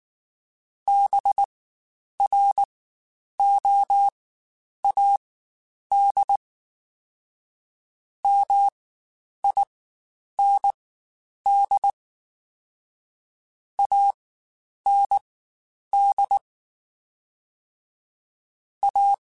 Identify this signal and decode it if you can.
Morse code, heard as a keyed tone, about 19 wpm, spaced as for 9 wpm, 800 Hz. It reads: BROAD MIND AND A
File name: Spaced / Chuckles Spaced-mp3s / MiddleAge sp @18